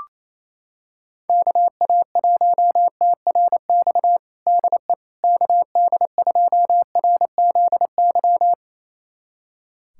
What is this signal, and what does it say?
Morse code, heard as a keyed tone, about 28 wpm, 700 Hz. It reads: KA1TRX DE KD2RZY
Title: USA Callsign DE Callsign - 6 Characters - 28wpm